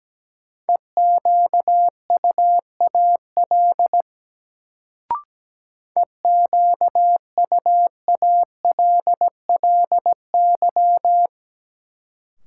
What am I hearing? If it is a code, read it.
Morse code, heard as a keyed tone, about 17 wpm, 700 Hz. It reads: EQUAL EQUALLY
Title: Sentences from Top 1400 Words - New Word Drill - 17wpm